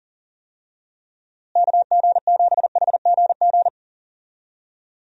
Morse code, heard as a keyed tone, about 40 wpm, 700 Hz. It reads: KG7HGG
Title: USA Call Signs - 6 Characters - Repeated 3x - 40wpm